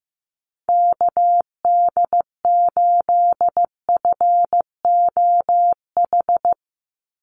Morse code, heard as a keyed tone, about 15 wpm, 700 Hz. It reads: KD8FOH
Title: USA Call Signs 15wpm